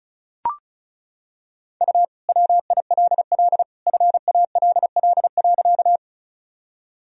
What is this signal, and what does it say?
Morse code, heard as a keyed tone, about 36 wpm, 700 Hz. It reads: U WILL FALL.
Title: Sentences from Top 500 Words and Abbreviations 35wpm